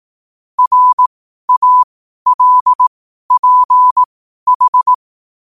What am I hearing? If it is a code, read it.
Morse code, heard as a keyed tone, about 18 wpm, spaced as for 12 wpm, 1000 Hz. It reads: RALPH
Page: New quotes every day in morse code at 12 Words per minute.